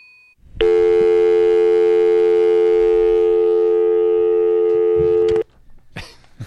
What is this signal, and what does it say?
Dial tone (North America), 350 + 440 Hz.